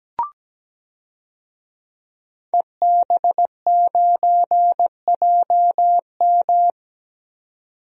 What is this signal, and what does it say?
Morse code, heard as a keyed tone, about 17 wpm, 700 Hz. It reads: EB9JM
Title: DXCC-100 Call Signs - 5 Characters - 17wpm